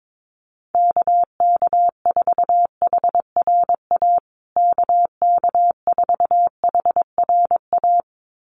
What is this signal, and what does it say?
Morse code, heard as a keyed tone, about 22 wpm, 700 Hz. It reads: KK4HRA KK4HRA
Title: USA Call Signs Repeated 2x 22wpm